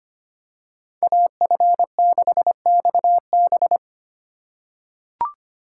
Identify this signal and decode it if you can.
Morse code, heard as a keyed tone, about 25 wpm, 700 Hz. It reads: AF6XB
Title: USA Call Signs - 5 Characters - 2x Character Spacing - 25wpm